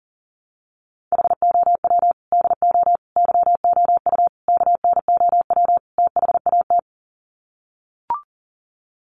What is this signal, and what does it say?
Morse code, heard as a keyed tone, about 40 wpm, 700 Hz. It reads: HOW DO YOU KNOW THAT
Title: Sets of 5 Words Encore 40wpm